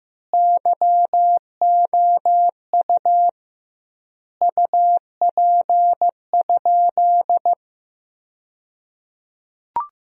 Morse code, heard as a keyed tone, about 15 wpm, 700 Hz. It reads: YOU UP?
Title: 2x Word-Spacing - Sentences from Top 500 Words - 15wpm